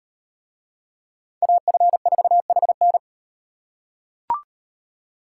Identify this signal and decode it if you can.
Morse code, heard as a keyed tone, about 36 wpm, 700 Hz. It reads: AF4HN